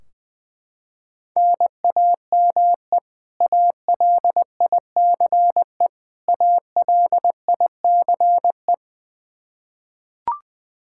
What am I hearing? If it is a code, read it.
Morse code, heard as a keyed tone, about 20 wpm, 700 Hz. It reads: NAME ALICE ALICE
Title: Practice QSO - Exchange 1 - Beginning - 20wpm